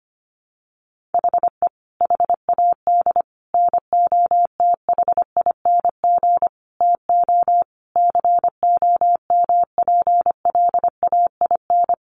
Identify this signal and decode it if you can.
Morse code, heard as a keyed tone, about 25 wpm, 700 Hz. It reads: HE HAD NOTHING TO COMPLAIN
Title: Sentences from Top 1300 Words 25wpm